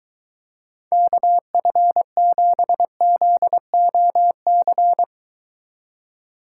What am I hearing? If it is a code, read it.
Morse code, heard as a keyed tone, about 23 wpm, 700 Hz. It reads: KF7ZOC